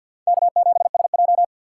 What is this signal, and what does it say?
Morse code, heard as a keyed tone, about 52 wpm, 700 Hz. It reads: K7SJ
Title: USA Call Signs - 3x Character Spacing - 50wpm